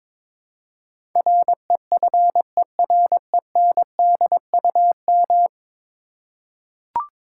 Morse code, heard as a keyed tone, about 21 wpm, 700 Hz. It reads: REFERENDUM
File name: practice / 22wpm / 10 Letter Words - First 3 Letters Given - 22wpm